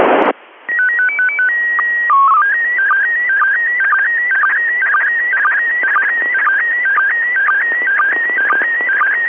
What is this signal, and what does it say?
SSTV (slow-scan television), PD 120.